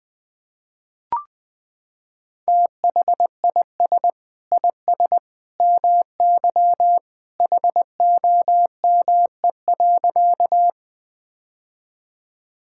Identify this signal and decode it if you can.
Morse code, heard as a keyed tone, about 19 wpm, 700 Hz. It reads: THIS IS MY HOME.
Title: Sentences with 2-Word Phrases - 20wpm